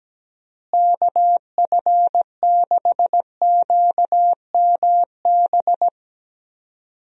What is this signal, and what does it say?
Morse code, heard as a keyed tone, about 17 wpm, 700 Hz. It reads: KF6QMB
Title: USA Call Signs - 6 Characters - Repeated 3x - 17wpm